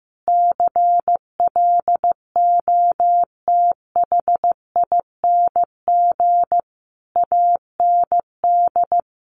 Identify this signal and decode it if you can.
Morse code, heard as a keyed tone, about 15 wpm, 700 Hz. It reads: CLOTHING AND
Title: A Sound Of Thunder 15wpm